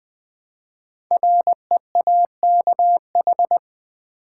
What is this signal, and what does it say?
Morse code, heard as a keyed tone, about 21 wpm, 700 Hz. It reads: REAKH
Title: Code Groups - 5 Letter:Numbers - 20wpm